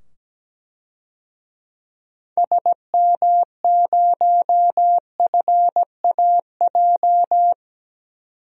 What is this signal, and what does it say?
Morse code, heard as a keyed tone, about 17 wpm, 700 Hz. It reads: SM0FAJ
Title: DXCC-100 Call Signs - 6 Characters - 17wpm